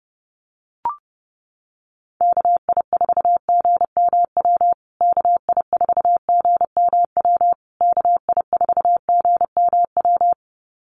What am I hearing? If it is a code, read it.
Morse code, heard as a keyed tone, about 30 wpm, 700 Hz. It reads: KI4GMW KI4GMW KI4GMW
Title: USA Call Signs Repeated 3x 30wpm